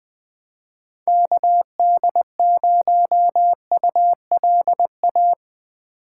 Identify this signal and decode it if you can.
Morse code, heard as a keyed tone, about 20 wpm, 700 Hz. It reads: KD0ULA
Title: USA Call Signs - 6 Characters - 4x Character Spacing - 20wpm